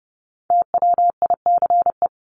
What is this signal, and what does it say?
Morse code, heard as a keyed tone, about 30 wpm, 700 Hz. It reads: TWICE
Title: Top 1501-1750 Words 30wpm